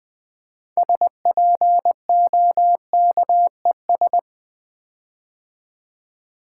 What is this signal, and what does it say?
Morse code, heard as a keyed tone, about 20 wpm, 700 Hz. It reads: SPOKES
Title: Build A Word - 6 Letters At A Time - 12-Letter Words - 20wpm